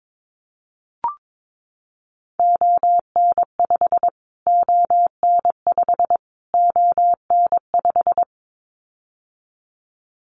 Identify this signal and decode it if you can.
Morse code, heard as a keyed tone, about 22 wpm, 700 Hz. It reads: ON5 ON5 ON5
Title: Callsign Prefixes for Top 25 DXCC - Repeated 3x - 22wpm